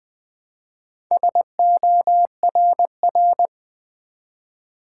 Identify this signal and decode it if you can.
Morse code, heard as a keyed tone, about 20 wpm, 700 Hz. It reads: SORR
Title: Build A Word - 1 Letter At A Time - 5-Letter Words - 20wpm